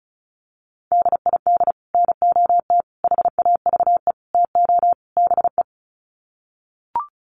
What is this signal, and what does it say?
Morse code, heard as a keyed tone, about 34 wpm, 700 Hz. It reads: DID NOT HAVE TO BE
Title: Sets of 5 Words Encore2 35wpm